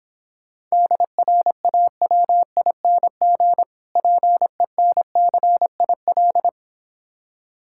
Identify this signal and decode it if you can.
Morse code, heard as a keyed tone, about 26 wpm, 700 Hz. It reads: DRAWING PENCIL